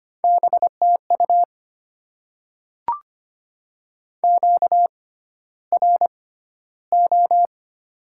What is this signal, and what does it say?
Morse code, heard as a keyed tone, about 25 wpm, spaced as for 13 wpm, 690 Hz. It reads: BTU QRO
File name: Top 90 QSO Elements - 6x Character Spacing - 25wpm